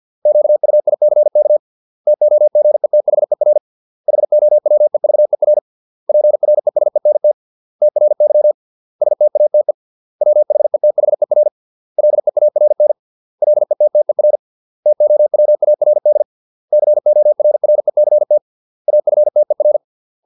Morse code, heard as a keyed tone, about 50 wpm, 600 Hz. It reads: QUICK TOGETHER HOWEVER PRESENT TRY STATE WHETHER LEARN LETTER TOWARD CORRECT AFTER